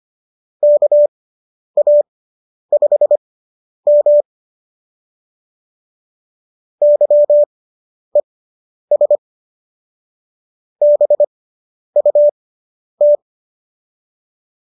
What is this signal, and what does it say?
Morse code, heard as a keyed tone, about 25 wpm, spaced as for 10 wpm, 570 Hz. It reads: KA5M YES BUT